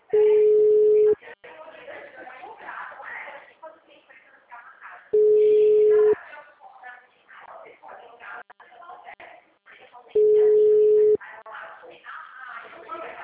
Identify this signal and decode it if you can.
Ringback tone (Europe, 425 Hz), with speech.